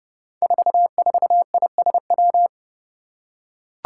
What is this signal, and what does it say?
Morse code, heard as a keyed tone, about 30 wpm, 700 Hz. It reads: 44ISW
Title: Five Challenging Letter-Number Combinations 30wpm